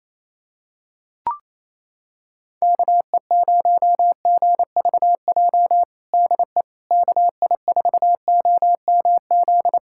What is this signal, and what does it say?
Morse code, heard as a keyed tone, about 28 wpm, 700 Hz. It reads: KE0GVJ DE KI4OMZ
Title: USA Callsign DE Callsign 28wpm